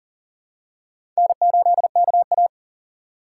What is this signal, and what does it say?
Morse code, heard as a keyed tone, about 40 wpm, 700 Hz. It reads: N8KA
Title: USA Call Signs - 4 Characters - 3x Character Spacing - 40wpm